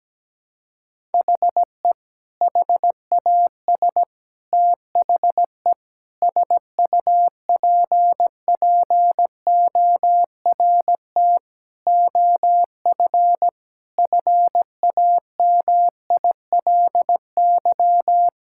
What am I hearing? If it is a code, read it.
Morse code, heard as a keyed tone, about 17 wpm, 700 Hz. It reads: HE HAS THE SUPPORT OF FAMILY
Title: Sentences from Top 1000 Words - Review Part 5 - 17wpm